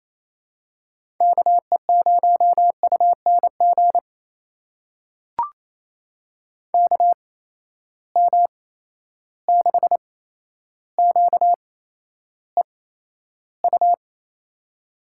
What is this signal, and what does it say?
Morse code, heard as a keyed tone, about 28 wpm, spaced as for 12 wpm, 700 Hz. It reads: KE0UNG KM6QEU
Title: USA Call Signs - 6 Characters - 8x Character Spacing - 28wpm